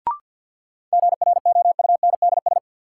Morse code, heard as a keyed tone, about 51 wpm, 700 Hz. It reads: GROUNDS
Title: Sentences from Top 1300 Words - New Word Drill Alt - 50wpm